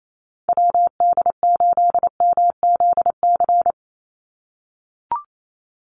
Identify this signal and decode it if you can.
Morse code, heard as a keyed tone, about 28 wpm, 700 Hz. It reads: WD8MZC
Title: USA Call Signs 28wpm